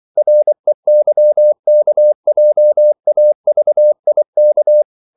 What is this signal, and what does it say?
Morse code, heard as a keyed tone, about 24 wpm, 600 Hz. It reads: REYKJAVIK
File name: CW / Practice / NationalCapitals24